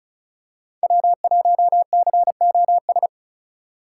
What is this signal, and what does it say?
Morse code, heard as a keyed tone, about 36 wpm, 700 Hz. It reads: W1COS
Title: USA Call Signs - 5 Characters - 2x Character Spacing - 35wpm